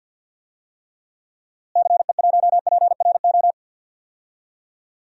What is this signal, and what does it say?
Morse code, heard as a keyed tone, about 50 wpm, 700 Hz. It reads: CE1PRO
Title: DXCC-100 Call Signs - 6 Characters - Repeated 2x - 50wpm